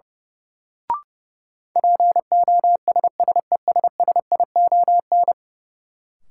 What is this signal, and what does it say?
Morse code, heard as a keyed tone, about 32 wpm, 700 Hz. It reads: POSSESSION
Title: Sentences from Top 1400 Words - New Word Drill - 30wpm